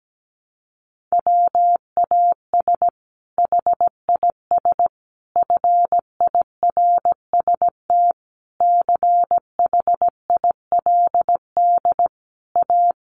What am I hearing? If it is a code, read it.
Morse code, heard as a keyed tone, about 17 wpm, 700 Hz. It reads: WAS HIS FIRST CHILD A
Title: Sentences from Top 800 Words 17wpm